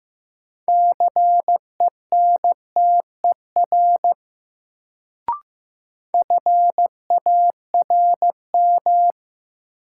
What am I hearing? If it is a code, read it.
Morse code, heard as a keyed tone, about 15 wpm, 700 Hz. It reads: CENTER FARM
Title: Sentences from Top 1000 Words - Review Part 2 - New Word Drill - 15wpm